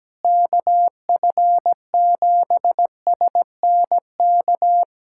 Morse code, heard as a keyed tone, about 17 wpm, 700 Hz. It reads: KF7SNK
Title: USA Callsign DE Callsign 17wpm